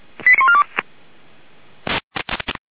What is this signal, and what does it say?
Five-tone selective call (ZVEI), heard as quick runs of five tones, tones of about 70 ms. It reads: 89142